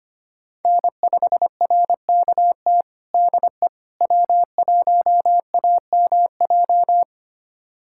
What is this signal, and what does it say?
Morse code, heard as a keyed tone, about 25 wpm, 700 Hz. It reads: N5RKT DE W1AMJ
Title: USA Callsign DE Callsign - 5 Characters - 25wpm